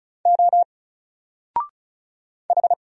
Morse code, heard as a keyed tone, about 34 wpm, 700 Hz. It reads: O H
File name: Single-Letter-Number-35wpm